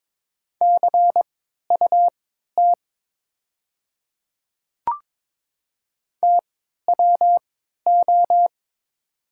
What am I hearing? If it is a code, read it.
Morse code, heard as a keyed tone, about 22 wpm, spaced as for 12 wpm, 700 Hz. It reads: CUT TWO
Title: 3x Character Spacing - Top 300 Words - 22wpm